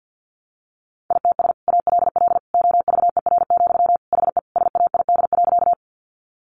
Morse code, heard as a keyed tone, about 50 wpm, 700 Hz. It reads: ITS ALL OVER, HE SAID.
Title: Sentences from Top 600 Words 50wpm